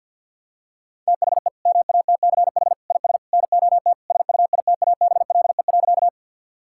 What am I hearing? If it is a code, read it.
Morse code, heard as a keyed tone, about 50 wpm, 700 Hz. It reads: THE MATCH IS NOT SUITABLE.
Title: Sentences from Top 1600 Words 50wpm